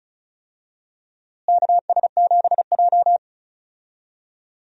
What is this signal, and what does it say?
Morse code, heard as a keyed tone, about 34 wpm, 700 Hz. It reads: KS7J
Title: USA Call Signs - 4 Characters - Repeated 2x - 35wpm